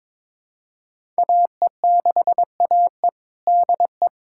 Morse code, heard as a keyed tone, about 22 wpm, 700 Hz. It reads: AE6AE DE
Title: USA Callsign DE Callsign - 5 Characters - 22wpm